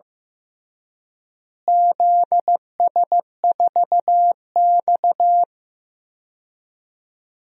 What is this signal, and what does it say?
Morse code, heard as a keyed tone, about 15 wpm, 700 Hz. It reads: ZS4X
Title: DXCC-100 Call Signs - 4 Characters - Repeated 3x - 15wpm